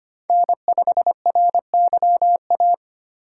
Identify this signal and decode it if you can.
Morse code, heard as a keyed tone, about 25 wpm, 700 Hz. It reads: N5RYA
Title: USA Call Signs - 3x Character Spacing - 25wpm